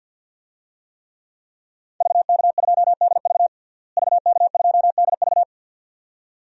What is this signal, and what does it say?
Morse code, heard as a keyed tone, about 52 wpm, 700 Hz. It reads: VK2DV VK2DV
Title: DXCC-100 Call Signs - 5 Characters - Repeated 2x - 50wpm